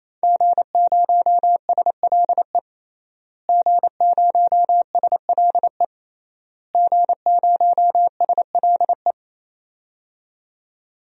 Morse code, heard as a keyed tone, about 28 wpm, 700 Hz. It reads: G0SLE G0SLE G0SLE
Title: DXCC-100 Call Signs - 5 Characters - Repeated 3x - 28wpm